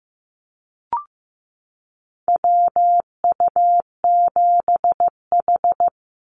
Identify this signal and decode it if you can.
Morse code, heard as a keyed tone, about 15 wpm, 700 Hz. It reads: WU7H
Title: NA SOTA Activator Callsigns 15wpm